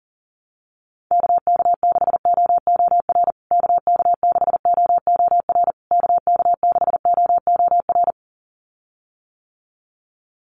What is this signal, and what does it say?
Morse code, heard as a keyed tone, about 40 wpm, 700 Hz. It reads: KK6OOR KK6OOR KK6OOR
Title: USA Call Signs Repeated 3x 40wpm